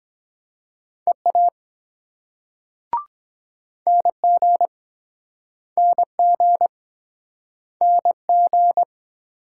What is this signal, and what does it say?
Morse code, heard as a keyed tone, about 23 wpm, 700 Hz. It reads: EA NG NG NG